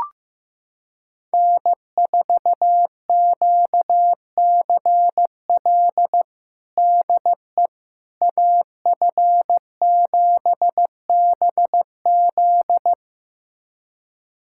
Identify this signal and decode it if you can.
Morse code, heard as a keyed tone, about 15 wpm, 700 Hz. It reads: N4QCL DE AF7BZ
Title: USA Callsign DE Callsign - 5 Characters - 15wpm